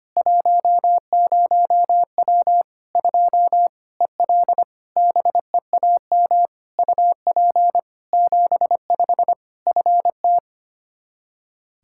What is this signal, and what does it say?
Morse code, heard as a keyed tone, about 25 wpm, 700 Hz. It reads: 10W 2 EL BEAM UP 75 FT
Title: K1USN SST QSO Practice 25wpm